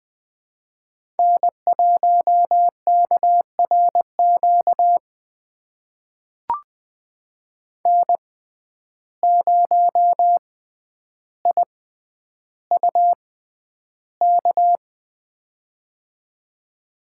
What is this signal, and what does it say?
Morse code, heard as a keyed tone, about 20 wpm, spaced as for 10 wpm, 700 Hz. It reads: N1KRQ N0IUK
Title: USA Call Signs - 5 Characters - 6x Character Spacing - 20wpm